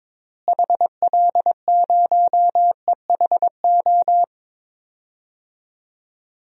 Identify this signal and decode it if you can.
Morse code, heard as a keyed tone, about 22 wpm, 700 Hz. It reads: HL0EHO
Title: DXCC-100 Call Signs - 6 Characters - Repeated 3x - 22wpm